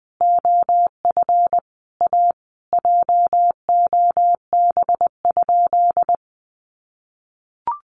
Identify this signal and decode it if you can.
Morse code, heard as a keyed tone, about 20 wpm, 700 Hz. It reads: OF A JOB?
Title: Sentences from Top 1300 Words 20wpm